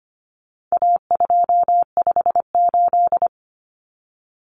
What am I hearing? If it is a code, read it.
Morse code, heard as a keyed tone, about 25 wpm, 700 Hz. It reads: A258
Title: Common Cut Numbers - 4-Digit 25wpm